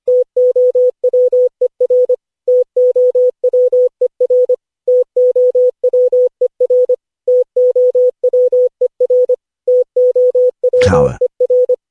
Morse code, heard as a keyed tone, about 25 wpm, 500 Hz. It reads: TOWER TOWER TOWER TOWER TOWER
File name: cw / Session5 cw training